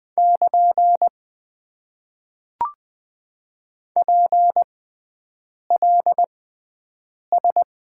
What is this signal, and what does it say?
Morse code, heard as a keyed tone, about 20 wpm, spaced as for 7 wpm, 700 Hz. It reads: <KN> PLS
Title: Top 90 QSO Elements - 6x Character Spacing - 20wpm